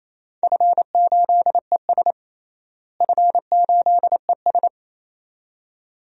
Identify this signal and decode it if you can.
Morse code, heard as a keyed tone, about 28 wpm, 700 Hz. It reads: F8ES F8ES
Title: DXCC-100 Call Signs - 4 Characters - Repeated 2x - 28wpm